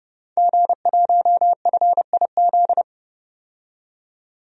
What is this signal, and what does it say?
Morse code, heard as a keyed tone, about 30 wpm, 700 Hz. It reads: G1FIZ
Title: DXCC-100 Call Signs - 5 Characters - 30wpm